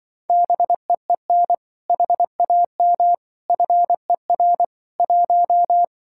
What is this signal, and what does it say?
Morse code, heard as a keyed tone, about 24 wpm, 700 Hz. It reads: BEEN HAM FER 1
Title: Speed-Racing Common QSO Phrases - 20wpm